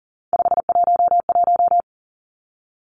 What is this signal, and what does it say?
Morse code, heard as a keyed tone, about 40 wpm, 700 Hz. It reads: 511
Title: Three Challenging Letter-Number Combinations 40wpm